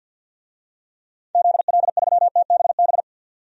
Morse code, heard as a keyed tone, about 53 wpm, 700 Hz. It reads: ZL3TBB
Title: DXCC-100 Call Signs - 6 Characters - 50wpm